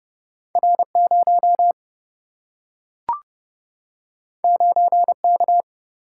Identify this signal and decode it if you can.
Morse code, heard as a keyed tone, about 30 wpm, 700 Hz. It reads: R0 9K
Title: Single-Letter Single-Number 30wpm